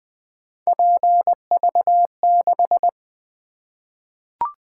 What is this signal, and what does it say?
Morse code, heard as a keyed tone, about 20 wpm, 700 Hz. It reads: PV6
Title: Three Letter-Number Combinations - 20wpm